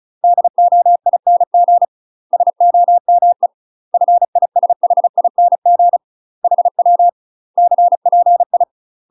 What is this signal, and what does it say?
Morse code, heard as a keyed tone, about 34 wpm, 700 Hz. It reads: DOING SOME FISHING HW CPI
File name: UK MASTER QSOs TRAINING/35WPM